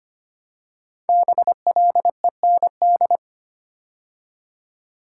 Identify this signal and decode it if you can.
Morse code, heard as a keyed tone, about 25 wpm, 700 Hz. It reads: BLEND
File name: 5 Letter Words - First 3 Letters Given - 25wpm